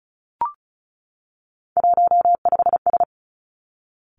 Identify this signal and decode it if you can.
Morse code, heard as a keyed tone, about 36 wpm, 700 Hz. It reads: 15S
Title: Three Challenging Letter-Number Combinations 35wpm